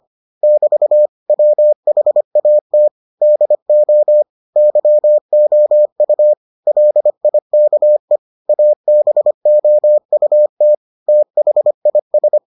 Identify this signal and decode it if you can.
Morse code, heard as a keyed tone, about 25 wpm, 600 Hz. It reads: <BT> WHAT DO YOU LIKE ABOUT THIS